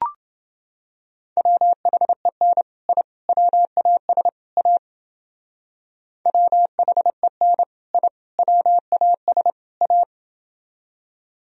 Morse code, heard as a keyed tone, about 29 wpm, 700 Hz. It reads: WHEN I WAS A WHEN I WAS A